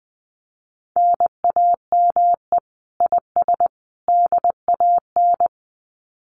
Morse code, heard as a keyed tone, about 20 wpm, 700 Hz. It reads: NAME IS DAN
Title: Common QSO Phrases 20wpm